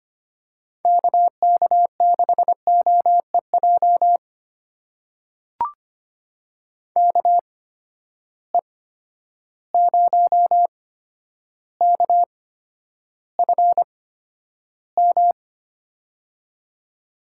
Morse code, heard as a keyed tone, about 25 wpm, spaced as for 11 wpm, 700 Hz. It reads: KK6OEJ KE0KFM
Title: USA Call Signs - 6 Characters - 8x Character Spacing - 25wpm